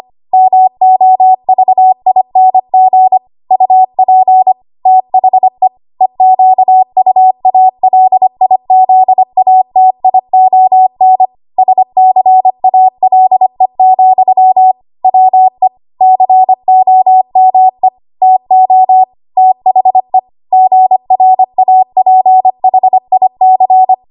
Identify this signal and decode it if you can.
Morse code, heard as a keyed tone, about 25 wpm, 750 Hz. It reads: MOVING UP THE EQUALIZATION SCALE, WE COME TO THE GRAPHIC
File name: Morse / 25WPM